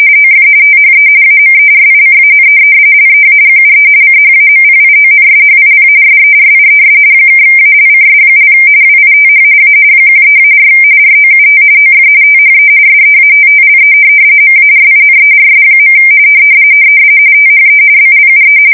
RTTY, about 45.45 baud. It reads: OF THE AMERICAN LEAGUE CHAMPIONSHIP SERIES (ALCS) WITH A 13-4 ROUT OF THE SEATTLE MARINERS. THE BLUE
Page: 170 Shift 2125/2295